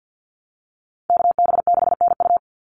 Morse code, heard as a keyed tone, about 50 wpm, 700 Hz. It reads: KB6NU
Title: Call Signs - 50wpm